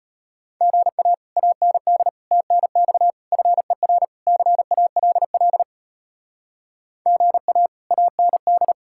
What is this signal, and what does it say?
Morse code, heard as a keyed tone, about 33 wpm, 700 Hz. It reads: GA AND TNX FER CALL GA AND